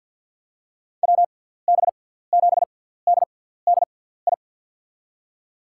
Morse code, heard as a keyed tone, about 47 wpm, spaced as for 17 wpm, 700 Hz. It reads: WB7DDI
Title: USA Call Signs - 6 Characters - 6x Character Spacing - 50wpm